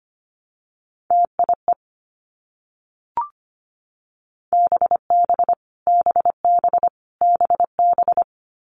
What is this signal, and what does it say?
Morse code, heard as a keyed tone, about 25 wpm, 700 Hz. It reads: TIE BB BB BB
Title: Multi-letter Phonemes Repeated 3x 25wpm